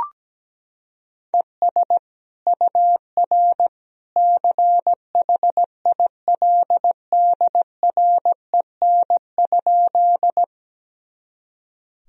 Morse code, heard as a keyed tone, about 17 wpm, 700 Hz. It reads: ES UR CHILDREN?
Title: Sentences from Top 300 Words and Abbreviations 17wpm